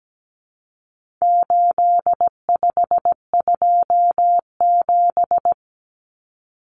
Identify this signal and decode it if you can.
Morse code, heard as a keyed tone, about 17 wpm, 700 Hz. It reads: 8527
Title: Common Cut Numbers - 4-Digit 17wpm